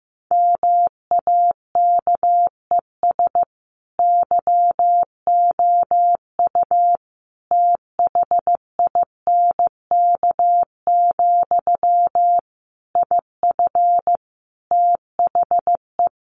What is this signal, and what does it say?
Morse code, heard as a keyed tone, about 15 wpm, 700 Hz. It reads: MAKES YOU THINK, IF THE
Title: A Sound Of Thunder 15wpm